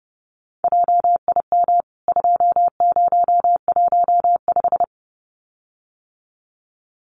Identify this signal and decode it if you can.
Morse code, heard as a keyed tone, about 30 wpm, 700 Hz. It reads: JIM 2015
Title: CWT Exchange 30wpm